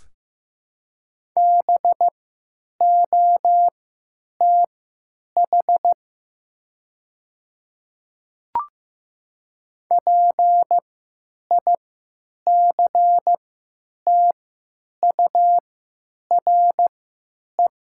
Morse code, heard as a keyed tone, about 15 wpm, spaced as for 9 wpm, 700 Hz. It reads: BOTH PICTURE
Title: 3x Character Spacing - Top 300 Words - 15wpm